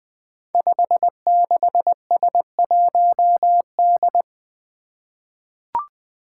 Morse code, heard as a keyed tone, about 21 wpm, 700 Hz. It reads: 56S1D
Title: Five Challenging Letter-Number Combinations 20wpm